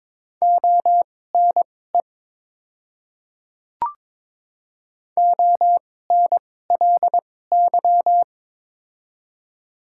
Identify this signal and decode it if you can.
Morse code, heard as a keyed tone, about 22 wpm, spaced as for 16 wpm, 700 Hz. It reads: ONE ONLY
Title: 2x Character Spacing - Top 200 Words - 22wpm